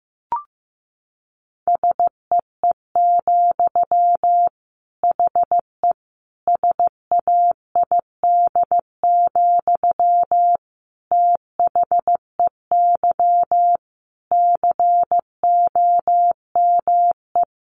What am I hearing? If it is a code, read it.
Morse code, heard as a keyed tone, about 15 wpm, 700 Hz. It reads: SEE, HE SAID, THEY COME
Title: Sentences from Top 700 Words 15wpm